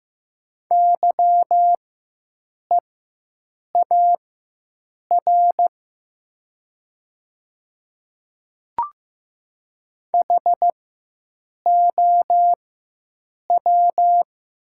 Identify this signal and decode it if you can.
Morse code, heard as a keyed tone, about 15 wpm, spaced as for 7 wpm, 700 Hz. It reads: YEAR HOW
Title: 4x Character Spacing - Top 200 Words - 15wpm